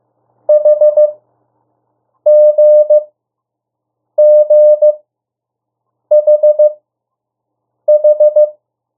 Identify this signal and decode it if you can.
Morse code, heard as a keyed tone, about 15 wpm, spaced as for 6 wpm, 600 Hz. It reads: HGGHH
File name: morse mp3 / 02 urok 1